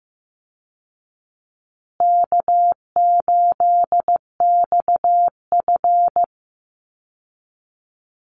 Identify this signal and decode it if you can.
Morse code, heard as a keyed tone, about 15 wpm, 700 Hz. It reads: K8XF
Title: Easy Call Signs Repeated 2x - 15wpm